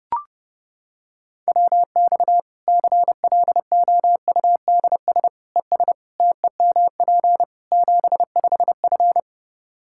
Morse code, heard as a keyed tone, about 30 wpm, 700 Hz. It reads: WX CLOUDS ES TEMP 75F
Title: Practice QSO - Exchange 2 - Info - 30wpm